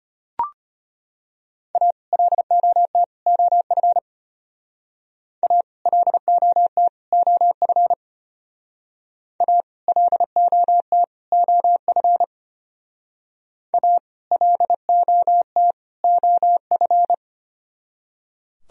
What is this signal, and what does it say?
Morse code, heard as a keyed tone, about 31 wpm, 700 Hz. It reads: A LOT OF A LOT OF A LOT OF A LOT OF
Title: Speed-Racing - Sets of 3 Words 25wpm